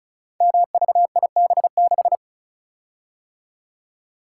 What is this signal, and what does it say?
Morse code, heard as a keyed tone, about 36 wpm, 690 Hz. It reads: MVIB6
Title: Code Groups - 5 Letter:Numbers - 35wpm